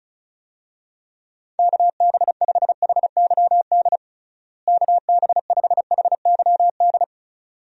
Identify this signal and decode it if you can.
Morse code, heard as a keyed tone, about 34 wpm, 700 Hz. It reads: KB5HYD KB5HYD
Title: USA Call Signs - 6 Characters - Repeated 2x - 35wpm